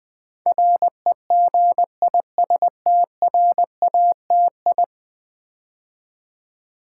Morse code, heard as a keyed tone, about 20 wpm, 700 Hz. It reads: REGISTRATI
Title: Build A Word - 5 Letters At A Time - 12-Letter Words - 20wpm